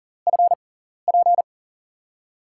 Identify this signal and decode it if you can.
Morse code, heard as a keyed tone, about 40 wpm, spaced as for 14 wpm, 700 Hz. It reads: FP
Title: USA Call Signs - 5 Characters - 6x Character Spacing - 40wpm